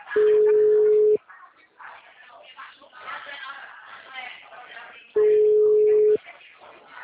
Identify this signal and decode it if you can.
Ringback tone (Europe, 425 Hz), with speech.
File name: out- -2005- - - .57779